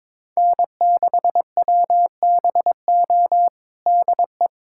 Morse code, heard as a keyed tone, about 22 wpm, 700 Hz. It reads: N6WBO DE
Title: USA Callsign DE Callsign - 5 Characters - 22wpm